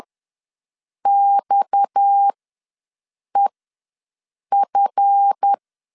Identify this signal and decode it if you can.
Morse code, heard as a keyed tone, about 10 wpm, spaced as for 6 wpm, 800 Hz. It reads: XEF
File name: LEZIONI / 8 YJ45 / 8 YJ ws 10wpm 4043